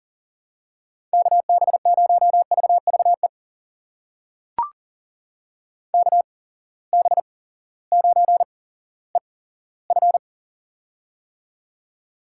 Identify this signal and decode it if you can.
Morse code, heard as a keyed tone, about 40 wpm, spaced as for 18 wpm, 700 Hz. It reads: KB0VVE KB9EF
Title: USA Call Signs - 8x Character Spacing - 40wpm